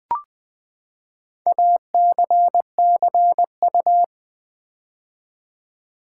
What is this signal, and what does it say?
Morse code, heard as a keyed tone, about 20 wpm, 700 Hz. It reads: ACCU
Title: Build A Word - 4 Letters At A Time - 12-Letter Words - 20wpm